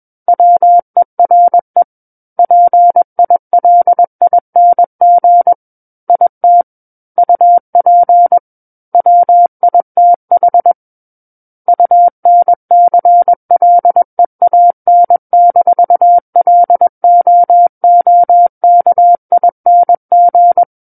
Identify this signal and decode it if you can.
Morse code, heard as a keyed tone, about 21 wpm, 700 Hz. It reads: WERE PILING IT UP WITH UNCLEAN-LOOKING
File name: Book One / 12-Chapter 12 - 21 WPM